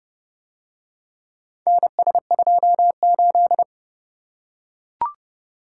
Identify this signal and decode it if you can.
Morse code, heard as a keyed tone, about 28 wpm, 700 Hz. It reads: NS28